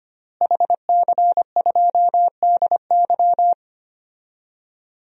Morse code, heard as a keyed tone, about 25 wpm, 700 Hz. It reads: HC2DY
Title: DXCC-100 Call Signs - 5 Characters - Repeated 3x - 25wpm